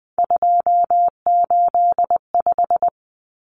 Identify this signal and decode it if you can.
Morse code, heard as a keyed tone, about 20 wpm, 700 Hz. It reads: 285
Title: Common Cut Numbers - 3-Digit 20wpm